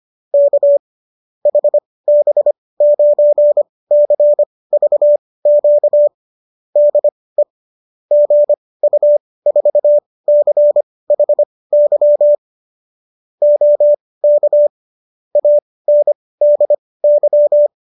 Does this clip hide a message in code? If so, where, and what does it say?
Yes: Morse code from 0.3 to 17.7 s, heard as a keyed tone, about 25 wpm, spaced as for 18 wpm, 590 Hz. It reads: K HB9CVQ DE GU4CHY OK ANDY